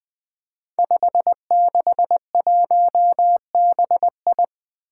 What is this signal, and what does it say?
Morse code, heard as a keyed tone, about 21 wpm, 700 Hz. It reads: 561BI
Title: Five Challenging Letter-Number Combinations 20wpm